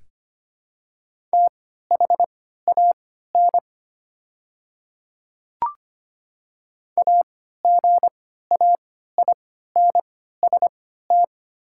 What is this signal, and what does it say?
Morse code, heard as a keyed tone, about 25 wpm, spaced as for 14 wpm, 700 Hz. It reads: THAN AGAINST
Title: 3x Character Spacing - Top 500 Words - 25wpm